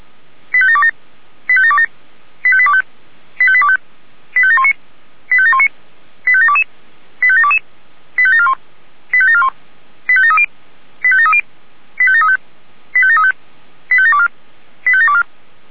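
Five-tone selective call (ZVEI), heard as quick runs of five tones, tones of about 70 ms. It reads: 85727 85727 85825 85825 85719 85719 85722 85722 85741 85741 85730 85730 85735 85735 85724 85724
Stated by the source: ZVEI five-tone selective call received on SDR.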